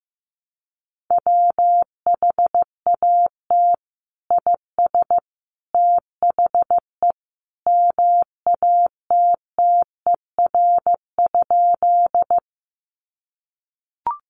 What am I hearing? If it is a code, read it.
Morse code, heard as a keyed tone, about 15 wpm, 700 Hz. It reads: WHAT IS THE MATTER?
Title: Sentences from Top 700 Words 15wpm